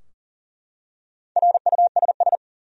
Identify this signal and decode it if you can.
Morse code, heard as a keyed tone, about 43 wpm, 700 Hz. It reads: RUSS
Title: USA Common Names - Max 4 Letters - 40wpm